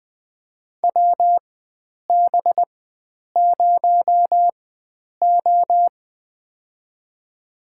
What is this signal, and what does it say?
Morse code, heard as a keyed tone, about 20 wpm, spaced as for 9 wpm, 700 Hz. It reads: WB0O
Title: USA Call Signs - 4 Characters - 4x Character Spacing - 20wpm